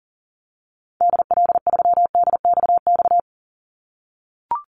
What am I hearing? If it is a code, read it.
Morse code, heard as a keyed tone, about 40 wpm, 700 Hz. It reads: DL3DXX
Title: Call Signs - 40wpm